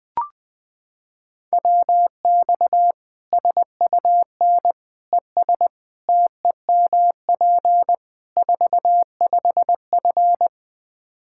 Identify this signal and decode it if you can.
Morse code, heard as a keyed tone, about 21 wpm, 700 Hz. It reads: WX SUN ES TEMP 45F
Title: Practice QSO - Exchange 2 - Info - 20wpm